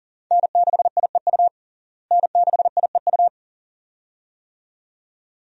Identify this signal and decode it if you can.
Morse code, heard as a keyed tone, about 40 wpm, 700 Hz. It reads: N6IEU N6IEU
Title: USA Call Signs - 5 Characters - Repeated 2x - 40wpm